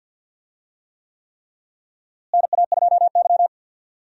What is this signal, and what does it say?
Morse code, heard as a keyed tone, about 45 wpm, 700 Hz. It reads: NA2Y
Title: USA Call Signs - 4 Characters - Repeated 3x - 50wpm